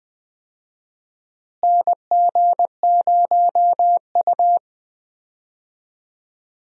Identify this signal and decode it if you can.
Morse code, heard as a keyed tone, about 21 wpm, 700 Hz. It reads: NG0U
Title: USA Call Signs - 4 Characters - Repeated 3x - 20wpm